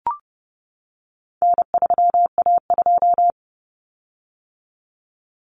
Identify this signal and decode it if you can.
Morse code, heard as a keyed tone, about 30 wpm, 700 Hz. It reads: N3A2
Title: Common Cut Numbers - 4-Digit 30wpm